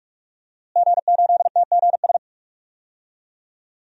Morse code, heard as a keyed tone, about 46 wpm, 700 Hz. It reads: G8TGS
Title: DXCC-100 Call Signs - 5 Characters - Repeated 2x - 45wpm